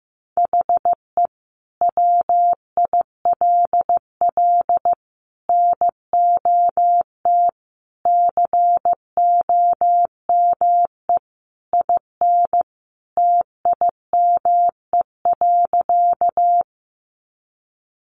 Morse code, heard as a keyed tone, about 15 wpm, 700 Hz. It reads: HE WILL NOT COME IN TIME.
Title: Sentences from Top 600 Words 15wpm